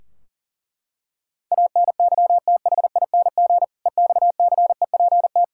Morse code, heard as a keyed tone, about 40 wpm, 700 Hz. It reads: ANYTHING EXCEPT
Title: Sentences from Top 1600 Words 40wpm